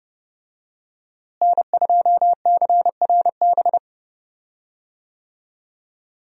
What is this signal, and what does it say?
Morse code, heard as a keyed tone, about 30 wpm, 700 Hz. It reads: N2CRB
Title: USA Call Signs - 5 Characters - Repeated 2x - 30wpm